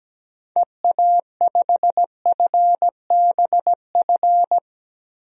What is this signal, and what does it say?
Morse code, heard as a keyed tone, about 17 wpm, 700 Hz. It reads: EA5FBF
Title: DXCC-100 Call Signs - 6 Characters - 17wpm